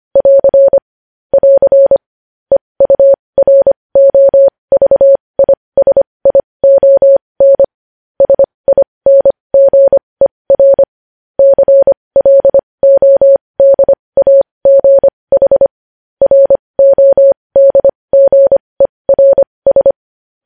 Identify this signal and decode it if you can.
Morse code, heard as a keyed tone, about 25 wpm, spaced as for 20 wpm, 550 Hz. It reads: <AR> <AR> EUROVISION SINGER CLODAGH RODGERS
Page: news-headlines-in-morse-code-at-20-wpm